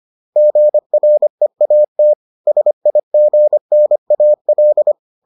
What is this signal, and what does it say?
Morse code, heard as a keyed tone, about 25 wpm, 600 Hz. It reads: GREAT SIGNAL